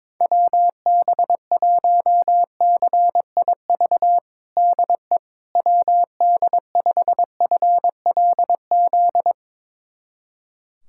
Morse code, heard as a keyed tone, about 22 wpm, 700 Hz. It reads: WB1CIV DE WD5FLZ
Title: USA Callsign DE Callsign - 6 Characters - 22wpm